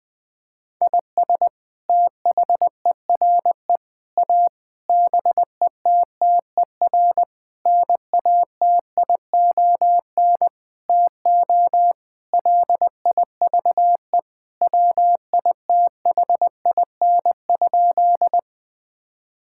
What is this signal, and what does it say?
Morse code, heard as a keyed tone, about 20 wpm, 700 Hz. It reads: IS THERE A BETTER NATION TO LIVE WITHIN?
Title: Sentences from Top 1000 Words - Review Part 6 - 20wpm